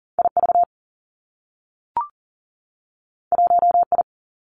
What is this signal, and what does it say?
Morse code, heard as a keyed tone, about 39 wpm, 700 Hz. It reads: I V 1 I
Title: Two Challenging Letter-Number Combinations 40wpm